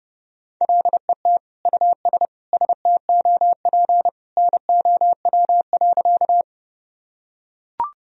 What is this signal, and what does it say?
Morse code, heard as a keyed tone, about 30 wpm, 700 Hz. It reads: LET US STOP NOW.
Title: Sentences from Top 1000 Words - Review Part 1 - 30wpm